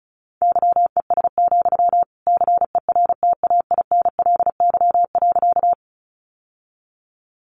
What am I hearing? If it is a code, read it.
Morse code, heard as a keyed tone, about 34 wpm, 700 Hz. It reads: YES, CERTAINLY.
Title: Sentences from Top 900 Words 35wpm